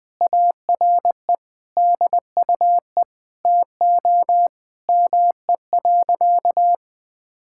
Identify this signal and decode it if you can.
Morse code, heard as a keyed tone, about 21 wpm, 700 Hz. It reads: ARE DUE TO ME.
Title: Sentences from Top 1000 Words - Review Part 6 - 20wpm